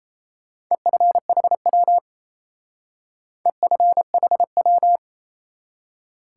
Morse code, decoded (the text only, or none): EFHW EFHW